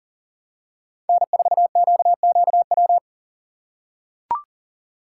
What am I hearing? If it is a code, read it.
Morse code, heard as a keyed tone, about 40 wpm, 700 Hz. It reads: N4QQW
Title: USA Call Signs - 5 Characters - 2x Character Spacing - 40wpm